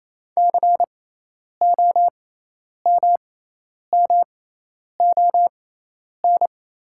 Morse code, heard as a keyed tone, about 28 wpm, spaced as for 10 wpm, 700 Hz. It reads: COMMON
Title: 6x Character Spacing - Top 500 Words - 28wpm